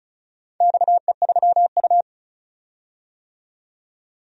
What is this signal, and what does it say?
Morse code, heard as a keyed tone, about 36 wpm, 700 Hz. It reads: XE3U
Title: DXCC-100 Call Signs - 4 Characters - 35wpm